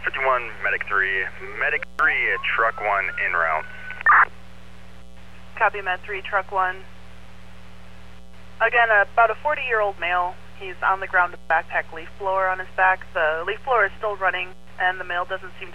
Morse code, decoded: none

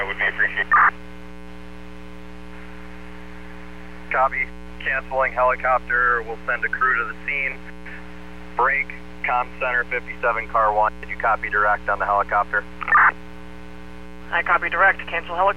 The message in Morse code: none

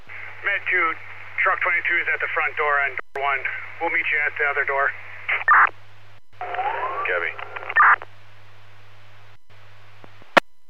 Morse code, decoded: none

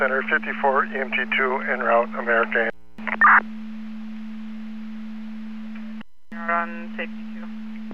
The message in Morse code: none

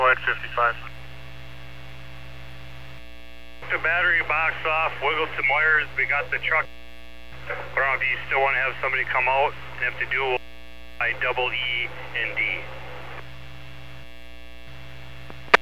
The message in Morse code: none

